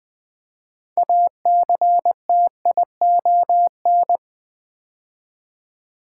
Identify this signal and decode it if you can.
Morse code, heard as a keyed tone, about 20 wpm, 700 Hz. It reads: ACTION
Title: 6 Letter Words - First 3 Letters Given - 20wpm